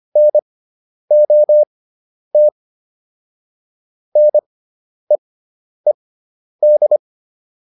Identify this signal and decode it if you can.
Morse code, heard as a keyed tone, about 25 wpm, spaced as for 10 wpm, 600 Hz. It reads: NOT NEED